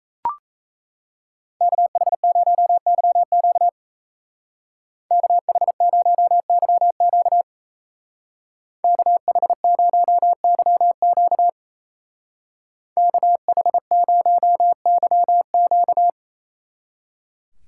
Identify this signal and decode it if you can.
Morse code, heard as a keyed tone, about 34 wpm, 700 Hz. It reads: KH0YQ KH0YQ KH0YQ KH0YQ
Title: Speed Racing - USA 2x2 Call Signs 28wpm